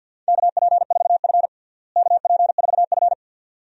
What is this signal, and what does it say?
Morse code, heard as a keyed tone, about 49 wpm, 700 Hz. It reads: KP4F KP4F
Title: DXCC-100 Call Signs - 4 Characters - Repeated 2x - 50wpm